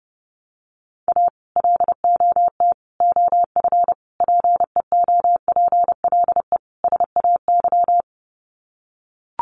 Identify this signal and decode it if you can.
Morse code, heard as a keyed tone, about 30 wpm, 700 Hz. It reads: A LOT OF PEOPLE SAY
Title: Sets of 5 Words Encore2 30wpm